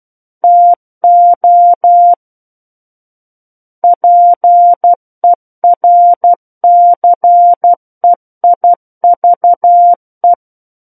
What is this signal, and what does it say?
Morse code, heard as a keyed tone, about 12 wpm, 700 Hz. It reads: TO PERCEIVE